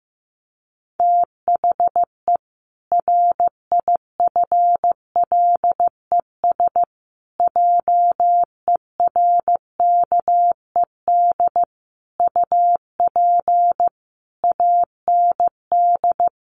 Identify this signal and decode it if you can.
Morse code, heard as a keyed tone, about 15 wpm, 700 Hz. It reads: THE RIFLES JERKED UP AND
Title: A Sound Of Thunder 15wpm